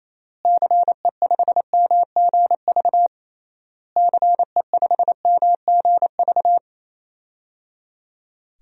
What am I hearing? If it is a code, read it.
Morse code, heard as a keyed tone, about 28 wpm, 700 Hz. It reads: CE5MGV CE5MGV
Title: DXCC-100 Call Signs - 6 Characters - Repeated 3x - 28wpm